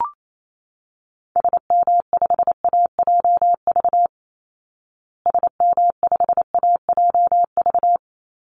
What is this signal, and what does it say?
Morse code, heard as a keyed tone, about 28 wpm, 700 Hz. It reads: SM5AJV SM5AJV
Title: Call Signs Repeated 3x - 28wpm